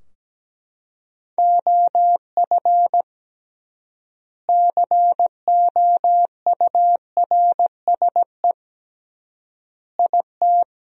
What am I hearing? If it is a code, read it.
Morse code, heard as a keyed tone, about 17 wpm, 700 Hz. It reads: OF COURSE IT
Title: 3x Word-Spacing - Sentences from Top 500 Words - 17wpm